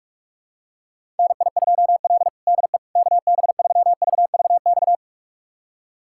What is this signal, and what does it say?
Morse code, heard as a keyed tone, about 46 wpm, 700 Hz. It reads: NI2L DE KB3VVX
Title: USA Callsign DE Callsign 45wpm